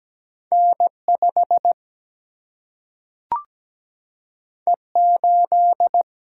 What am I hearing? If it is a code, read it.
Morse code, heard as a keyed tone, about 17 wpm, 700 Hz. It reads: N5 E8
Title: Single-Letter Single-Number 17wpm